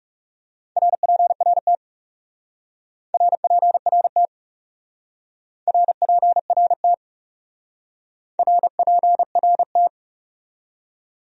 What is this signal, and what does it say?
Morse code, heard as a keyed tone, about 39 wpm, 700 Hz. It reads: RPRT RPRT RPRT RPRT